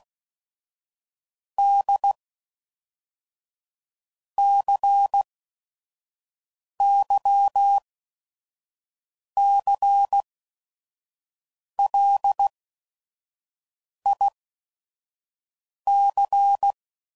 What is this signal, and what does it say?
Morse code, heard as a keyed tone, about 16 wpm, 800 Hz. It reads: D C Y C L I C